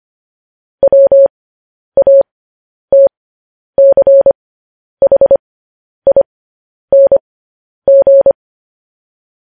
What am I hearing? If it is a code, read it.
Morse code, heard as a keyed tone, about 25 wpm, spaced as for 10 wpm, 550 Hz. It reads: WATCHING
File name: podcast / News headlines-1x-25x10